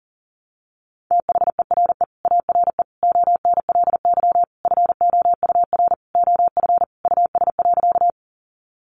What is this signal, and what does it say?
Morse code, heard as a keyed tone, about 40 wpm, 700 Hz. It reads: THERE ARE ONLY FOUR OF US.
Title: Sentences from Top 300 Words 40wpm